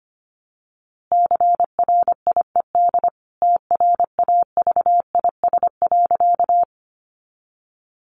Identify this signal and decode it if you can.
Morse code, heard as a keyed tone, about 25 wpm, 700 Hz. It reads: CRIED TRAVIS.
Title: A Sound Of Thunder 25wpm